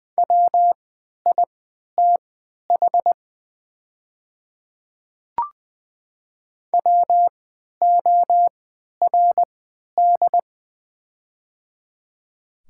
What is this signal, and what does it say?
Morse code, heard as a keyed tone, about 21 wpm, spaced as for 12 wpm, 690 Hz. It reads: WITH WORD
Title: 3x Character Spacing - Top 100 Words - 20wpm